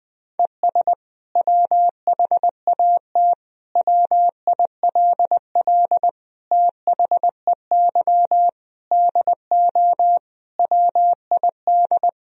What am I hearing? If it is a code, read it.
Morse code, heard as a keyed tone, about 21 wpm, 700 Hz. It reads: ES WHAT WILL THEY DO WID
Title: Sentences from Top 200 Words and Abbreviations 20wpm